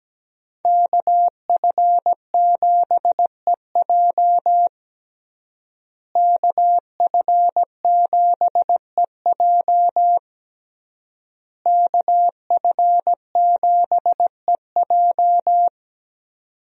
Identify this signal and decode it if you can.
Morse code, heard as a keyed tone, about 17 wpm, 700 Hz. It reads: KF7EJ KF7EJ KF7EJ
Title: USA Call Signs - 5 Characters - Repeated 3x - 17wpm